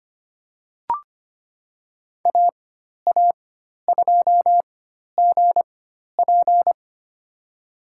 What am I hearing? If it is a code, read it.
Morse code, heard as a keyed tone, about 25 wpm, spaced as for 12 wpm, 700 Hz. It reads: AA2GP
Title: USA Call Signs - 5 Characters - 4x Character Spacing - 25wpm